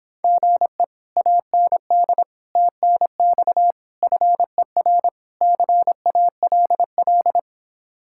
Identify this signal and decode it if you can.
Morse code, heard as a keyed tone, about 26 wpm, 700 Hz. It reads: GE AND TNX FER CALL